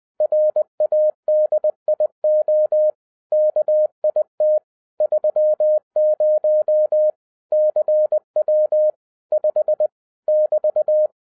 Morse code, heard as a keyed tone, about 20 wpm, 610 Hz. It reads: RADIO KIT 30 CW 5 <BT>
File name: 2011-07-mp3 / n23 QRP OTAKU 20wpm